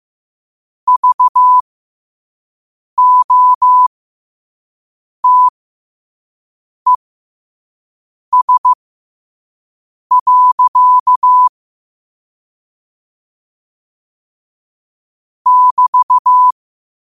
Morse code, heard as a keyed tone, about 15 wpm, spaced as for 5 wpm, 1000 Hz. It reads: VOTES. <BT>
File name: qotdfiles / 5wpm